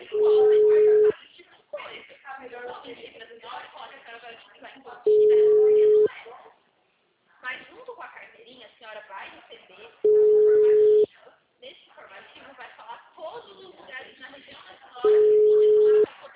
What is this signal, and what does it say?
Ringback tone (Europe, 425 Hz), with speech.